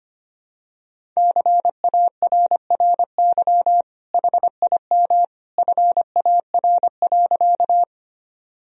Morse code, heard as a keyed tone, about 25 wpm, 700 Hz. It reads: CARRY HIM FAR.
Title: Sentences from Top 1000 Words - Review Part 1 - 25wpm